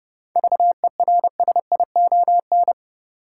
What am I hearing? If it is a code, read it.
Morse code, heard as a keyed tone, about 30 wpm, 700 Hz. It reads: VERSION
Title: Seven Letter Words - 30wpm